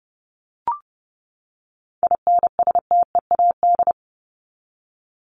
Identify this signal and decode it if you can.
Morse code, heard as a keyed tone, about 30 wpm, 700 Hz. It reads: INSTEAD
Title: Top 501-750 Words 30wpm